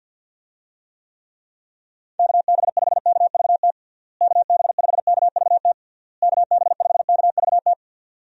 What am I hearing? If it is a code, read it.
Morse code, heard as a keyed tone, about 46 wpm, 700 Hz. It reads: KB5KVT KB5KVT KB5KVT
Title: USA Call Signs - 6 Characters - Repeated 3x - 50wpm